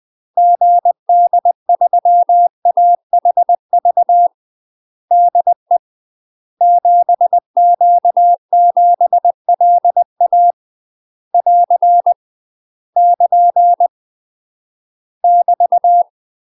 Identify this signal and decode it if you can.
Morse code, heard as a keyed tone, about 20 wpm, 700 Hz. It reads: GD3AHV DE 7Q7LA <AR> <KN> <BT>
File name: UK MASTER QSOs TRAINING / 20WPM / UK MASTER QSOs 0035